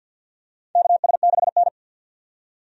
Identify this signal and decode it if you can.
Morse code, heard as a keyed tone, about 50 wpm, 700 Hz. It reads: KS6N
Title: USA Call Signs - 4 Characters - 50wpm